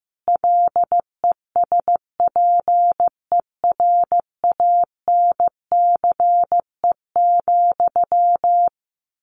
Morse code, heard as a keyed tone, about 15 wpm, 700 Hz. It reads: LESPERANCE,
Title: A Sound Of Thunder 15wpm